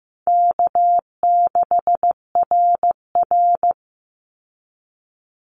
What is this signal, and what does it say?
Morse code, heard as a keyed tone, about 15 wpm, 700 Hz. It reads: K6RR
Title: Easy Call Signs Repeated 2x - 15wpm